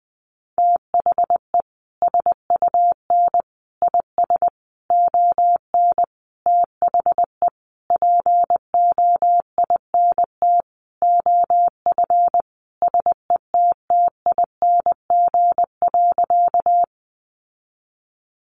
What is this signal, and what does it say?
Morse code, heard as a keyed tone, about 20 wpm, 700 Hz. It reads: THE SUN IS ON THE POINT OF SETTING.
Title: Sentences from Top 1300 Words 20wpm